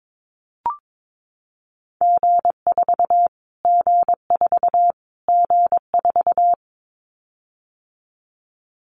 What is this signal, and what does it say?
Morse code, heard as a keyed tone, about 22 wpm, 700 Hz. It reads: G4 G4 G4
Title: Callsign Prefixes for Top 25 DXCC - Repeated 3x - 22wpm